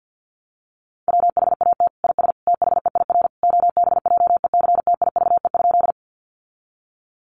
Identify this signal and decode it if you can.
Morse code, heard as a keyed tone, about 50 wpm, 700 Hz. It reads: WHAT IS THEIR OBJECTIVE?
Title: Sentences from Top 1200 Words 50wpm